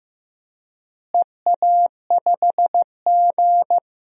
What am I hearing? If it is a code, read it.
Morse code, heard as a keyed tone, about 15 wpm, 700 Hz. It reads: EA5G
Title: DXCC-100 Call Signs - 4 Characters - Repeated 3x - 15wpm